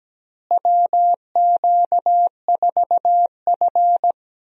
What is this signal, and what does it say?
Morse code, heard as a keyed tone, about 17 wpm, 700 Hz. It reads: WQ4F
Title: USA Call Signs - 4 Characters - Repeated 2x - 17wpm